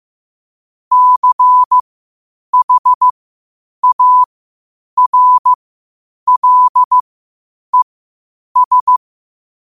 Morse code, heard as a keyed tone, about 15 wpm, spaced as for 8 wpm, 1000 Hz. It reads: CHARLES
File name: qotdfiles / 8wpm